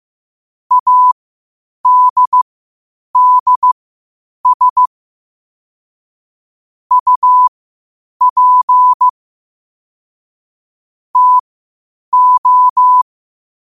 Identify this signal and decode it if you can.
Morse code, heard as a keyed tone, about 15 wpm, spaced as for 8 wpm, 1000 Hz. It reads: ADDS UP TO
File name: qotdfiles / 8wpm